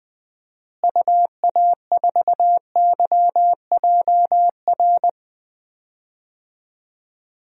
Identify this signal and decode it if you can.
Morse code, heard as a keyed tone, about 20 wpm, 700 Hz. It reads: UA4YJR
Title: DXCC-100 Call Signs - 6 Characters - Repeated 3x - 20wpm